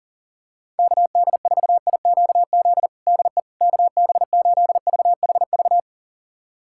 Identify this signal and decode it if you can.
Morse code, heard as a keyed tone, about 40 wpm, 700 Hz. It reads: KD4IQZ DE KB8VHV
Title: USA Callsign DE Callsign - 6 Characters - 40wpm